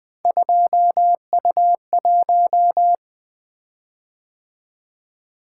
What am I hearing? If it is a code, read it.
Morse code, heard as a keyed tone, about 20 wpm, 700 Hz. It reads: 2U1
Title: Three Letter-Number Combinations - 20wpm